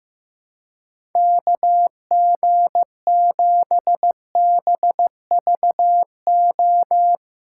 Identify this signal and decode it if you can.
Morse code, heard as a keyed tone, about 15 wpm, 700 Hz. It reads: KG7BVO
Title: USA Callsign DE Callsign - 6 Characters - 15wpm